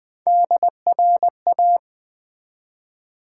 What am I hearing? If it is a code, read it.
Morse code, heard as a keyed tone, about 20 wpm, 700 Hz. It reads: DRA
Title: Build A Word - 1 Letter At A Time - 4-Letter Words - 20wpm